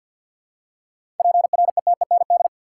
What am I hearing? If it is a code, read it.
Morse code, heard as a keyed tone, about 51 wpm, 700 Hz. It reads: PRETEND
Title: Seven Letter Words - 50wpm